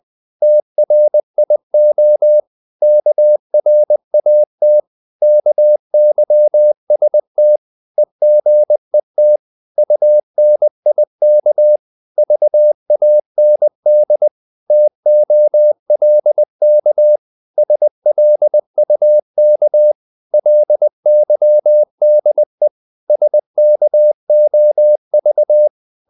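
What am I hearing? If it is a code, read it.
Morse code, heard as a keyed tone, about 20 wpm, 600 Hz. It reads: TRIO KRAT KYST EGET UNIK VAND TOLK SLUK LYDE SKOV
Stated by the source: Morse code practice of short Danish words at 20 wpm, from CW med Gnister.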